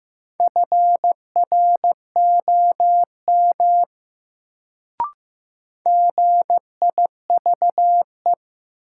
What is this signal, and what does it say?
Morse code, heard as a keyed tone, about 15 wpm, 700 Hz. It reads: FROM GIVE
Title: Sets of 2 Words - New Word Drill - 15wpm